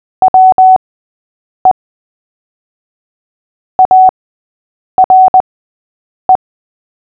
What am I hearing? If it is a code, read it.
Morse code, heard as a keyed tone, about 20 wpm, spaced as for 8 wpm, 750 Hz. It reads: WE ARE
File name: cw / fists / Short-Story-8-8wpm